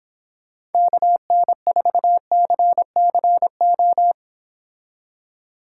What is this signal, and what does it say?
Morse code, heard as a keyed tone, about 28 wpm, 700 Hz. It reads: KN4CCO